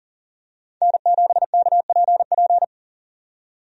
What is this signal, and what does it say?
Morse code, heard as a keyed tone, about 40 wpm, 700 Hz. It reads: N7KPP
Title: USA Call Signs - 5 Characters - 40wpm